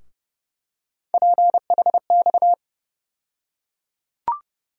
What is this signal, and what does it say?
Morse code, heard as a keyed tone, about 30 wpm, 700 Hz. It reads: PHX
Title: USA Call Sign Postfixes - 3 Letters 30wpm